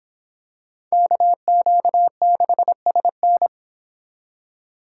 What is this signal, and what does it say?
Morse code, heard as a keyed tone, about 26 wpm, 700 Hz. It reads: KQ6SN